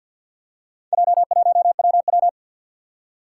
Morse code, heard as a keyed tone, about 54 wpm, 700 Hz. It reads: J1WW
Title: Four Challenging Letter-Number Combinations 50wpm